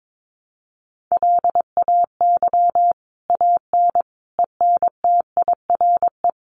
Morse code, heard as a keyed tone, about 22 wpm, 700 Hz. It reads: LAY AN ENTIRE
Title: A Sound Of Thunder 22wpm